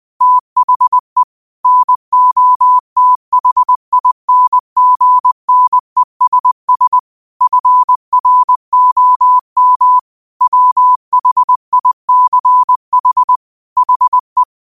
Morse code, heard as a keyed tone, about 20 wpm, 1000 Hz. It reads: THE NOTHINGNESS FROM WHICH HE